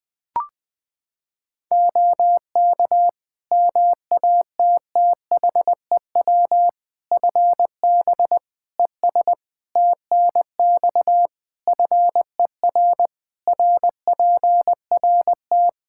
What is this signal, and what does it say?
Morse code, heard as a keyed tone, about 20 wpm, 700 Hz. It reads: OK MATTHEW FB ES TNX FER RPRT
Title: Practice QSO - Exchange 2 - Info - 20wpm